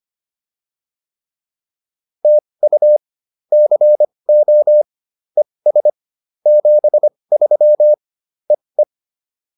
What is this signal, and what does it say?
Morse code, heard as a keyed tone, about 25 wpm, spaced as for 20 wpm, 590 Hz. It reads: TU CO ES 73 EE